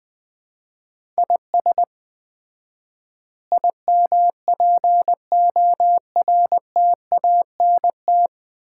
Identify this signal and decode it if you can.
Morse code, heard as a keyed tone, about 20 wpm, 700 Hz. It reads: IS IMPORTANT
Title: 4x Word-Spacing - Sets of 5 Words - 20wpm